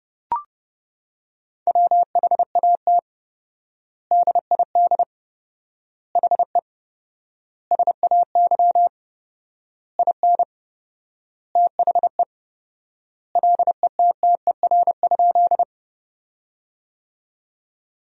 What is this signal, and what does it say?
Morse code, heard as a keyed tone, about 30 wpm, 700 Hz. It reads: WHAT DID HE SAY IN THE LETTER?
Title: 4x Word-Spacing - Sentences from Top 300 Words - 30wpm